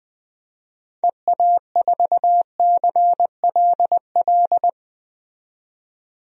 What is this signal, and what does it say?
Morse code, heard as a keyed tone, about 21 wpm, 700 Hz. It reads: EA4CLL
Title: DXCC-100 Call Signs - 6 Characters - 20wpm